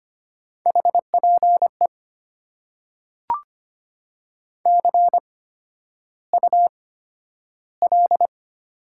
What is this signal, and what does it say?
Morse code, heard as a keyed tone, about 25 wpm, spaced as for 11 wpm, 700 Hz. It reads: HPE CUL
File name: Top 90 QSO Elements - 8x Character Spacing - 25wpm